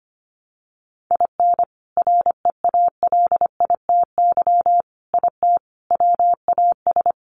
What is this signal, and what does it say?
Morse code, heard as a keyed tone, about 25 wpm, 700 Hz. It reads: IN REALITY IT WAS
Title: Sentences from Top 1200 Words 25wpm